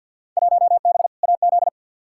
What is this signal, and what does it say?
Morse code, heard as a keyed tone, about 53 wpm, 700 Hz. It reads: 1B AZ
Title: Field Day Exchanges - 50wpm